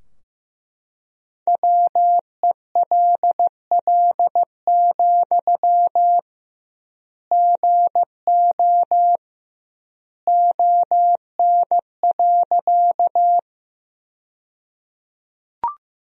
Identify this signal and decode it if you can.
Morse code, heard as a keyed tone, about 15 wpm, 700 Hz. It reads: WELL, GO ON.
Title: 2x Word-Spacing - Sentences from Top 200 Words - 15wpm